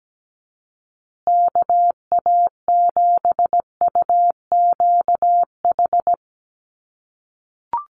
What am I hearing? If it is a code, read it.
Morse code, heard as a keyed tone, about 17 wpm, 700 Hz. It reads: KA7UQH
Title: USA Call Signs Repeated 2x 17wpm